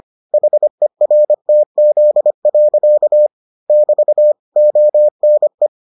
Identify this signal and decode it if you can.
Morse code, heard as a keyed tone, about 25 wpm, 600 Hz. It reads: HERTZ. <BT> ONE